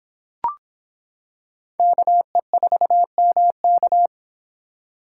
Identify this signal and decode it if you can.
Morse code, heard as a keyed tone, about 26 wpm, 700 Hz. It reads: KE4MK